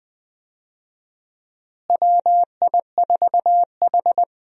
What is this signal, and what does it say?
Morse code, heard as a keyed tone, about 20 wpm, 700 Hz. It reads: WI4H
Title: USA Call Signs - 4 Characters - Repeated 2x - 20wpm